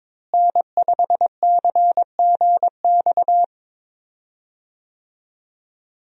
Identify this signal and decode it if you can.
Morse code, heard as a keyed tone, about 21 wpm, 700 Hz. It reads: N5CGX
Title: USA Call Signs - 5 Characters - Repeated 3x - 22wpm